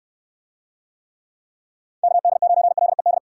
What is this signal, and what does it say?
Morse code, heard as a keyed tone, about 68 wpm, 700 Hz. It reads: KD9LED